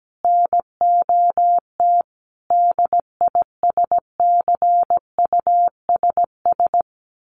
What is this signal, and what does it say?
Morse code, heard as a keyed tone, about 17 wpm, 700 Hz. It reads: NOT DISCUSS
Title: Sentences from Top 1100 Words 17wpm